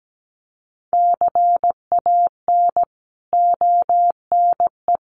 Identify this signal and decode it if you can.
Morse code, heard as a keyed tone, about 17 wpm, 700 Hz. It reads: CAN ONE
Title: Sentences from Top 100 Words 17wpm